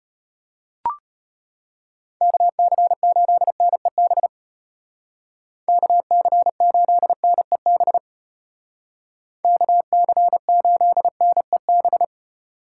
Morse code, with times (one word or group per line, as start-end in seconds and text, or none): 2.2-4.3 KC8NEB
5.7-8.0 KC8NEB
9.4-12.0 KC8NEB